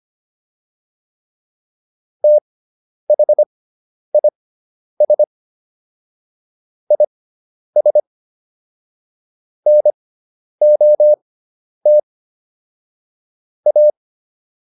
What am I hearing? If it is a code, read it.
Morse code, heard as a keyed tone, about 25 wpm, spaced as for 10 wpm, 600 Hz. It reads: THIS IS NOT A